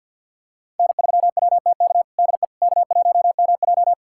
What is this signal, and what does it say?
Morse code, heard as a keyed tone, about 52 wpm, 700 Hz. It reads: N3WTK DE K1MJ
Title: USA Callsign DE Callsign 50wpm